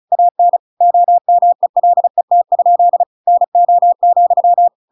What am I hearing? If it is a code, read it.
Morse code, heard as a keyed tone, about 34 wpm, 700 Hz. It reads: AN OMELET? NO,